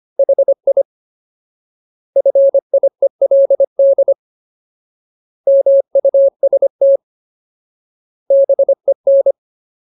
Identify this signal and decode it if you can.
Morse code, heard as a keyed tone, about 25 wpm, 550 Hz. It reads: HI FIELD MUST BEN
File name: Ultimate Word List / 25WPM Ultimate